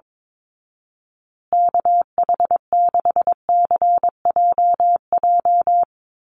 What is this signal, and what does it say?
Morse code, heard as a keyed tone, about 22 wpm, 700 Hz. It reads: KH6CJJ
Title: Call Signs Repeated 2x - 22wpm